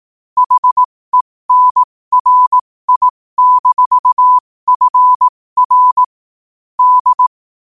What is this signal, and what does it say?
Morse code, heard as a keyed tone, about 18 wpm, spaced as for 15 wpm, 1000 Hz. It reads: HENRI-FR D
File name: qotdfiles / 15wpm